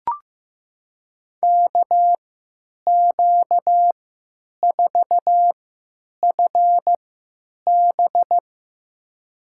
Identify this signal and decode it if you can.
Morse code, heard as a keyed tone, about 15 wpm, spaced as for 9 wpm, 700 Hz. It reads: KQ4FB
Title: USA Call Signs - 5 Characters - 3x Character Spacing - 15wpm